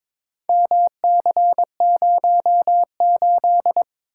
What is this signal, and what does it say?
Morse code, heard as a keyed tone, about 21 wpm, 700 Hz. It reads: MC08
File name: practice / 22wpm / Four Letter-Number Combinations - 22wpm